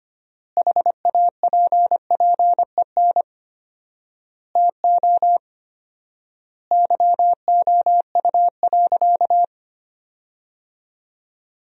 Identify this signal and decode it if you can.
Morse code, heard as a keyed tone, about 25 wpm, 700 Hz. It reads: HAPPEN TO YOU.
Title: 4x Word-Spacing - Sentences from Top 500 Words - 25wpm